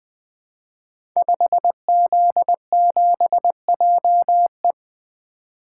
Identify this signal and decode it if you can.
Morse code, heard as a keyed tone, about 20 wpm, 700 Hz. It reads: 5Z7JE
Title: Code Groups - 5 Letter:Numbers - 20wpm